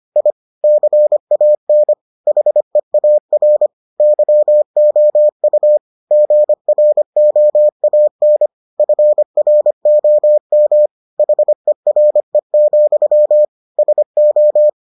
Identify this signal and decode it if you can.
Morse code, heard as a keyed tone, about 25 wpm, 600 Hz. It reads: I CAN HEAR YOU GROAN FROM HERE, SO